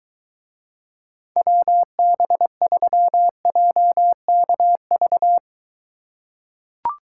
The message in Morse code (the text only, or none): WB3JKV